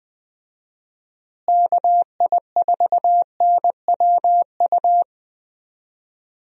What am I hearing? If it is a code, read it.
Morse code, heard as a keyed tone, about 20 wpm, 700 Hz. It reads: KI4NWU
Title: USA Call Signs - 6 Characters - Repeated 3x - 20wpm